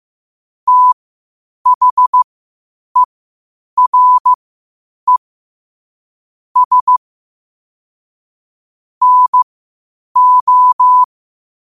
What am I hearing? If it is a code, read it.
Morse code, heard as a keyed tone, about 15 wpm, spaced as for 8 wpm, 1000 Hz. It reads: THERE S NO
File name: qotdfiles / 8wpm